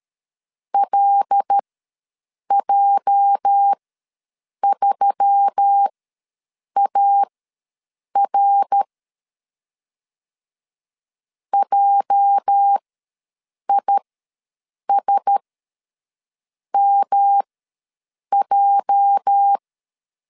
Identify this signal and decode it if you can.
Morse code, heard as a keyed tone, about 13 wpm, spaced as for 7 wpm, 800 Hz. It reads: LJ3AR JISMJ
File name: LEZIONI / 8 YJ45 / 8 YJ ws 12wpm 3342